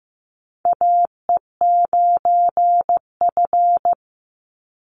Morse code, heard as a keyed tone, about 15 wpm, 700 Hz. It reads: AE9F
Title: NA SOTA Chaser Callsigns 15wpm